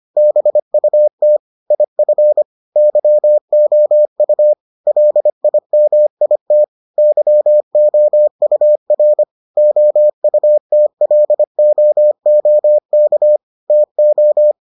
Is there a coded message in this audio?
Yes: Morse code from 0.2 to 14.5 s, heard as a keyed tone, about 25 wpm, 600 Hz. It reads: BUT IF YOU LIMIT YOUR OUTLOOK TO